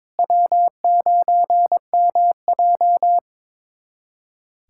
Morse code, heard as a keyed tone, about 22 wpm, 700 Hz. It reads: W9MJ
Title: USA Callsign DE Callsign - 4 Characters - 22wpm